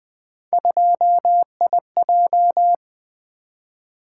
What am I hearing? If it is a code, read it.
Morse code, heard as a keyed tone, about 20 wpm, 700 Hz. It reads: 2IJ
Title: Three Letter-Number Combinations - 20wpm